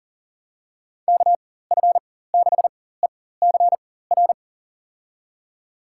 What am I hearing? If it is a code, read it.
Morse code, heard as a keyed tone, about 40 wpm, spaced as for 19 wpm, 700 Hz. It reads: KF6ECR
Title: USA Call Signs - 4x Character Spacing - 40wpm